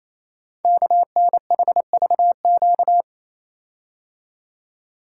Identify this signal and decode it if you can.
Morse code, heard as a keyed tone, about 27 wpm, 700 Hz. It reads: KNHVQ
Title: Code Groups - 5 Letters - 28wpm